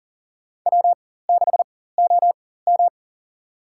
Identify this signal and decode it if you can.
Morse code, heard as a keyed tone, about 40 wpm, spaced as for 19 wpm, 700 Hz. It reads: W6OM
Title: USA Call Signs - 4 Characters - 4x Character Spacing - 40wpm